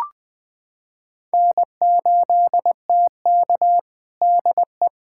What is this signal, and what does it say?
Morse code, heard as a keyed tone, about 20 wpm, 700 Hz. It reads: N8TK DE
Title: USA Callsign DE Callsign - 4 Characters - 20wpm